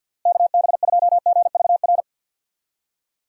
Morse code, heard as a keyed tone, about 50 wpm, 700 Hz. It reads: KB2GVR
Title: USA Call Signs - 6 Characters - Repeated 3x - 50wpm